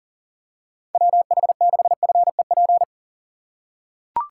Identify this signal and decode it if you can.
Morse code, heard as a keyed tone, about 40 wpm, 700 Hz. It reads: WH6FEP
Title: USA Call Signs - 6 Characters - 3x Character Spacing - 40wpm